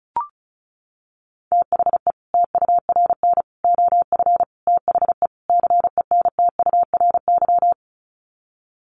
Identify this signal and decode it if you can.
Morse code, heard as a keyed tone, about 34 wpm, 700 Hz. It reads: THE TURN OF THE CENTURY
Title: Sets of 5 Words 35wpm